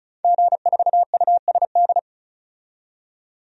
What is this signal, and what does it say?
Morse code, heard as a keyed tone, about 36 wpm, 700 Hz. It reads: G4USD
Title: DXCC-100 Call Signs - 5 Characters - 35wpm